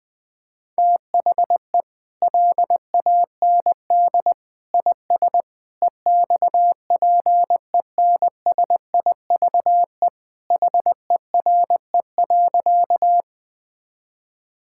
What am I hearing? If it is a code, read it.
Morse code, heard as a keyed tone, about 20 wpm, 700 Hz. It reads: THE LAND IS EXPENSIVE HERE.
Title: Sentences from Top 1000 Words - Review Part 6 - 20wpm